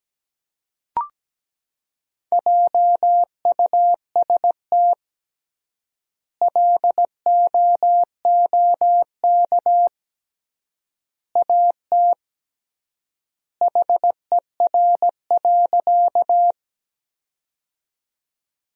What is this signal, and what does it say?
Morse code, heard as a keyed tone, about 17 wpm, 700 Hz. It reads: JUST LOOK AT HER.
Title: 3x Word-Spacing - Sentences from Top 200 Words - 17wpm